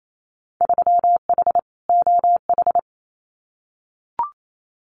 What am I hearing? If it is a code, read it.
Morse code, heard as a keyed tone, about 28 wpm, 700 Hz. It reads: 3H OH
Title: Winter Field Day Exchanges Encore 28wpm